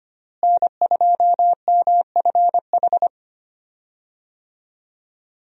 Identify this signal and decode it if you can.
Morse code, heard as a keyed tone, about 25 wpm, 700 Hz. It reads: N2MFH
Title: USA Call Signs - 5 Characters - Repeated 2x - 25wpm